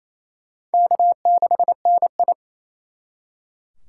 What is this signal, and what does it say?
Morse code, heard as a keyed tone, about 36 wpm, 700 Hz. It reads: K6NI